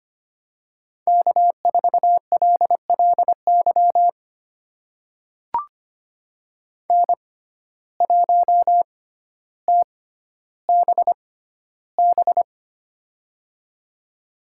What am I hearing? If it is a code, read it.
Morse code, heard as a keyed tone, about 25 wpm, spaced as for 13 wpm, 700 Hz. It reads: K4LLY N1TBB
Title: USA Call Signs - 5 Characters - 6x Character Spacing - 25wpm